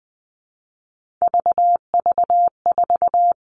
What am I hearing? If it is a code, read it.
Morse code, heard as a keyed tone, about 20 wpm, 700 Hz. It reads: VV4
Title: All Cut Numbers - 3-Digit 20wpm